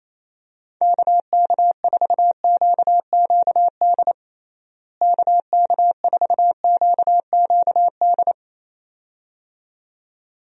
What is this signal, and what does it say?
Morse code, heard as a keyed tone, about 28 wpm, 700 Hz. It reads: KK4QQD KK4QQD
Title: USA Call Signs - 6 Characters - Repeated 2x - 28wpm